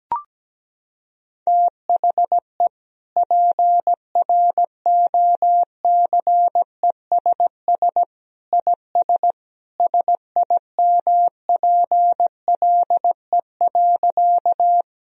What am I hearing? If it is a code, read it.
Morse code, heard as a keyed tone, about 17 wpm, 700 Hz. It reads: THE PROCESS IS SIMPLE.
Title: Sentences from Top 1000 Words - Review Part 3 - 17wpm